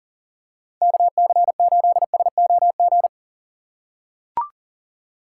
Morse code, heard as a keyed tone, about 41 wpm, 700 Hz. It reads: KC8SOG
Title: USA Call Signs - 6 Characters - 4x Character Spacing - 40wpm